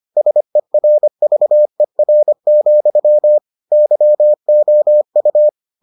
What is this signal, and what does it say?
Morse code, heard as a keyed tone, about 25 wpm, 600 Hz. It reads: SERVER, YOU